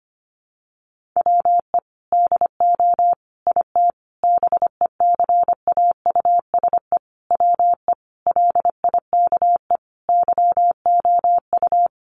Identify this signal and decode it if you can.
Morse code, heard as a keyed tone, about 25 wpm, 700 Hz. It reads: WE DO IT BECAUSE WE LIKE YOU
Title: Sentences from Top 1400 Words 25wpm